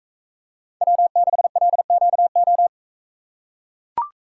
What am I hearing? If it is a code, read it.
Morse code, heard as a keyed tone, about 40 wpm, 700 Hz. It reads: W6LQO